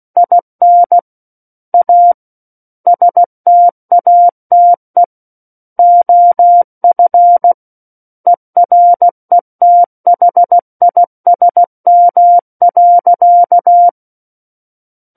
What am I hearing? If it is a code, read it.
Morse code, heard as a keyed tone, about 16 wpm, 700 Hz. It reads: IN A STATE OF ERETHISM.
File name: Book One / 07-Chapter 7 - 16 WPM